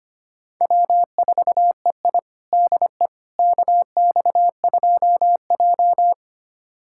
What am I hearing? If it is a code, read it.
Morse code, heard as a keyed tone, about 25 wpm, 700 Hz. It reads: W4EI DE KX2J
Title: USA Callsign DE Callsign - 4 Characters - 25wpm